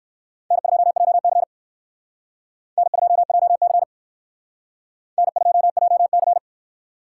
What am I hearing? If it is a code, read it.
Morse code, heard as a keyed tone, about 59 wpm, 700 Hz. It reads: N2JC N2JC N2JC